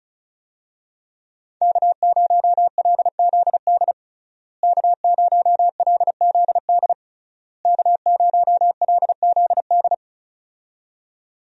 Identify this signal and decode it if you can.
Morse code, heard as a keyed tone, about 34 wpm, 700 Hz. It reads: K0LZD K0LZD K0LZD
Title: USA Call Signs - 5 Characters - Repeated 3x - 35wpm